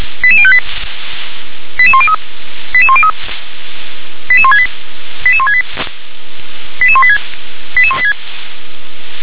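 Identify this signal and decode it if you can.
Five-tone selective call (ZVEI), heard as quick runs of five tones, tones of about 70 ms. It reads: 88058 88103 88103 88168 88168 88186 88186
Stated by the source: ZVEI five-tone selective call received with an SDR.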